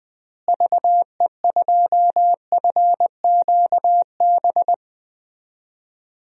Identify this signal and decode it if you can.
Morse code, heard as a keyed tone, about 20 wpm, 700 Hz. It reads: VE2FQB
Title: DXCC-100 Call Signs - 6 Characters - Repeated 2x - 20wpm